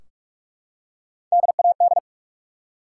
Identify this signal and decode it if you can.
Morse code, heard as a keyed tone, about 46 wpm, 700 Hz. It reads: DAD
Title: Three Letter Words - 45wpm